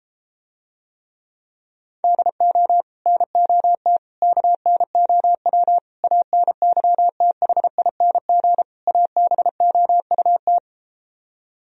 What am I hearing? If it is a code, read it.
Morse code, heard as a keyed tone, about 33 wpm, 700 Hz. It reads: DO NOT KNOW ANYTHING ABOUT